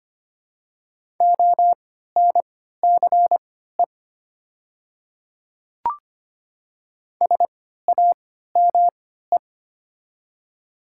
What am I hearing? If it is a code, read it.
Morse code, heard as a keyed tone, about 25 wpm, spaced as for 14 wpm, 700 Hz. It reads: ONCE SAME
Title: 3x Character Spacing - Top 500 Words - 25wpm